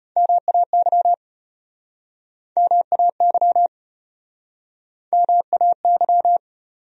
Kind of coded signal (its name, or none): Morse code